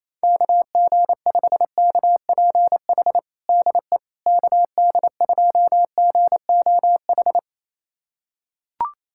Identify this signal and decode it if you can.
Morse code, heard as a keyed tone, about 28 wpm, 700 Hz. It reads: KG5KPH DE KD2GOH
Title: USA Callsign DE Callsign 28wpm